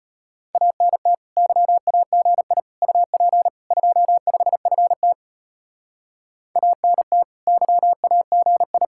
Morse code, heard as a keyed tone, about 30 wpm, 700 Hz. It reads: ANT YAGI UP 25FT ANT YAGI
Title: Speed-Racing Common QSO Phrases Encore - 25wpm